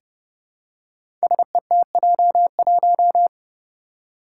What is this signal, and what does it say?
Morse code, heard as a keyed tone, about 30 wpm, 700 Hz. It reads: SETJ1
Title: Five Challenging Letter-Number Combinations 30wpm